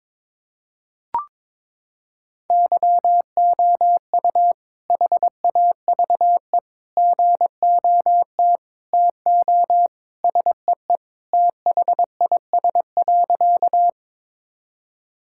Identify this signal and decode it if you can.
Morse code, heard as a keyed tone, about 22 wpm, 700 Hz. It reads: YOU HAVE GOT TO SEE THIS.
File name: practice / 22wpm / Sentences with 2-Word Phrases - 22wpm